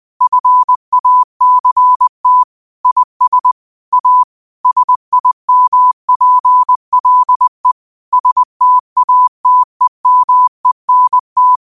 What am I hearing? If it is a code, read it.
Morse code, heard as a keyed tone, about 20 wpm, 1000 Hz. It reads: FACT IS A SIMPLE STATEMENT
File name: 20wpm